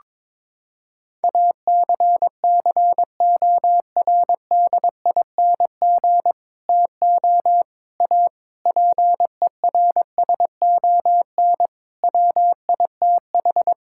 Morse code, heard as a keyed tone, about 22 wpm, 700 Hz. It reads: ACCORDING TO A PERSON WITH
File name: practice / 22wpm / Sets of 8 Words - 22wpm